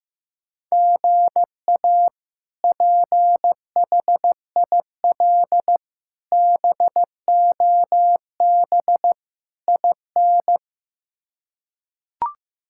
Morse code, heard as a keyed tone, about 16 wpm, 700 Hz. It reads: GA PHIL BOB IN
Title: K1USN SST QSO Practice - 15wpm